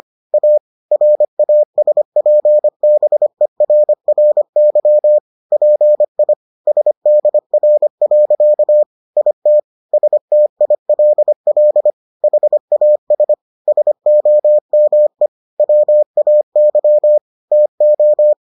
Morse code, heard as a keyed tone, about 25 wpm, 600 Hz. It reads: A RASPBERRY PI SDR. IT STILL HAS SOME WAY TO